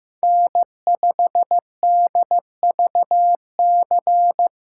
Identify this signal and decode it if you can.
Morse code, heard as a keyed tone, about 15 wpm, 700 Hz. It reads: N5DVC
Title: USA Call Signs - 5 Characters - 6x Character Spacing - 15wpm